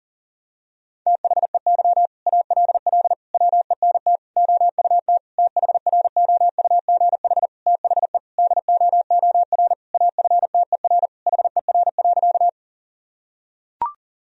Morse code, heard as a keyed tone, about 40 wpm, 700 Hz. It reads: THEY ALL WENT OUT THROUGH THE DOOR AFTER HER.
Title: Sentences from Top 1000 Words - Review Part 1 - 40wpm